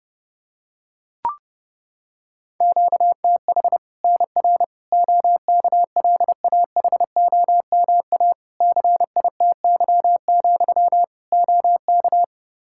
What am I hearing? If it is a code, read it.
Morse code, heard as a keyed tone, about 30 wpm, 700 Hz. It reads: QTH NR OKLAHOMA CITY, OK
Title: Practice QSO - Exchange 1 - Beginning - 30wpm